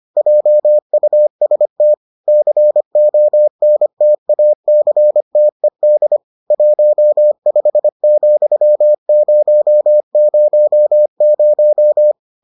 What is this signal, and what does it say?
Morse code, heard as a keyed tone, about 25 wpm, 600 Hz. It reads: JUST CONTACTED 15,000